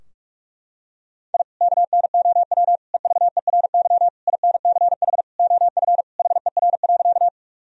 Morse code, heard as a keyed tone, about 46 wpm, 700 Hz. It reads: I KNOW EVERY INCH OF HER.
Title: Sentences from Top 1000 Words - Review Part 3 - 45wpm